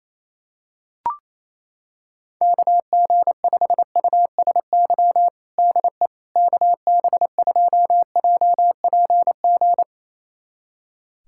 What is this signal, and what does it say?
Morse code, heard as a keyed tone, about 28 wpm, 700 Hz. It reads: KG5USY DE KB2JPG
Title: USA Callsign DE Callsign - 6 Characters - 28wpm